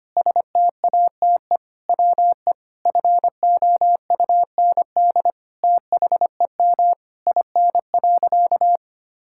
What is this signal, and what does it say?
Morse code, heard as a keyed tone, about 25 wpm, 700 Hz. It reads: STATE WE FOUND THEM IN.
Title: Sentences from Top 1000 Words - Review Part 6 - 25wpm